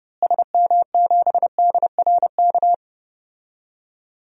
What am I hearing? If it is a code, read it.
Morse code, heard as a keyed tone, about 30 wpm, 700 Hz. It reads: SM7DRK
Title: DXCC-100 Call Signs - 6 Characters - Repeated 2x - 30wpm